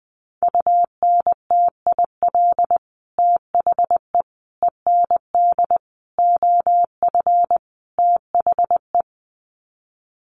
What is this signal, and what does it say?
Morse code, heard as a keyed tone, about 20 wpm, 700 Hz. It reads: UNTIL THE END OF THE
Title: Sets of 5 Words Encore 20wpm